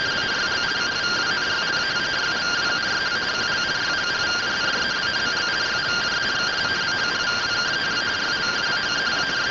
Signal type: two-tone teleprinter signal

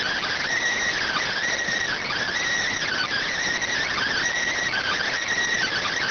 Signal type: Emergency Alert System (SAME) tones

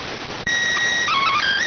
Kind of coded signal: SSTV (slow-scan television)

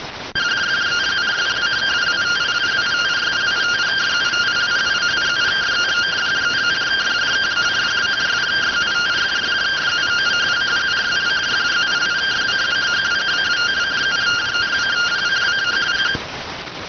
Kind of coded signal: two-tone teleprinter signal

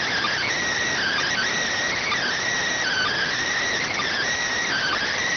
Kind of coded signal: Emergency Alert System (SAME) tones

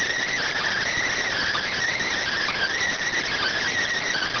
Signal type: Emergency Alert System (SAME) tones

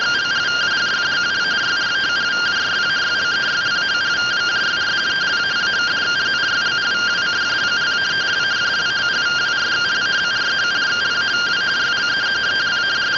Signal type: two-tone teleprinter signal

